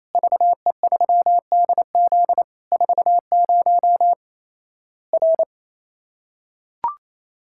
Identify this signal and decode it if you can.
Morse code, heard as a keyed tone, about 28 wpm, 700 Hz. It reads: VE3DZ 40 R
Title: Worked All Europe DX Contest - QTC Exchange - 28wpm